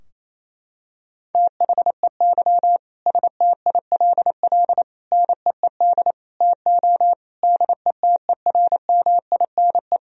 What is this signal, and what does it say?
Morse code, heard as a keyed tone, about 28 wpm, 700 Hz. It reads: THEY STILL NEED TO DETERMINE
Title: Sentences from Top 1000 Words - Review Part 4 - 28wpm